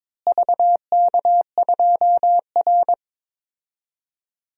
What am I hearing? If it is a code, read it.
Morse code, heard as a keyed tone, about 22 wpm, 700 Hz. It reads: VK2R
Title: DXCC-100 Call Signs - 4 Characters - Repeated 3x - 22wpm